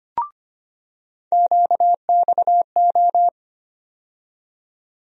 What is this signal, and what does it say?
Morse code, heard as a keyed tone, about 25 wpm, 700 Hz. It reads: QXO
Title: USA Call Sign Postfixes - 3 Letters 25wpm